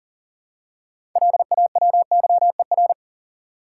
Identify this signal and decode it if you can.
Morse code, heard as a keyed tone, about 40 wpm, 700 Hz. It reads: LAWYER
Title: Six Letter Words - 40wpm